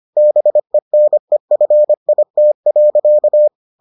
Morse code, heard as a keyed tone, about 25 wpm, 600 Hz. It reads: BENEFIT.